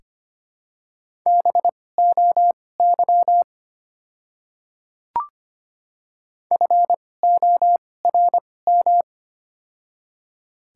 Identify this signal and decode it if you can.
Morse code, heard as a keyed tone, about 25 wpm, spaced as for 18 wpm, 700 Hz. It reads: BOY FORM
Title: 2x Character Spacing - Top 500 Words - 25wpm